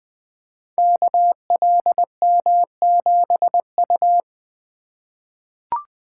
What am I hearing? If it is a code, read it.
Morse code, heard as a keyed tone, about 20 wpm, 700 Hz. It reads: KLM7U
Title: Code Groups - 5 Letter:Numbers - 20wpm